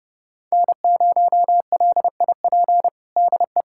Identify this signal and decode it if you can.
Morse code, heard as a keyed tone, about 32 wpm, 700 Hz. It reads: N0LIP DE
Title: USA Callsign DE Callsign - 5 Characters - 30wpm